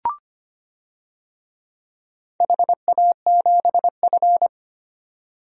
Morse code, heard as a keyed tone, about 25 wpm, 700 Hz. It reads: HA7F
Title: DXCC-100 Call Signs - 4 Characters - 25wpm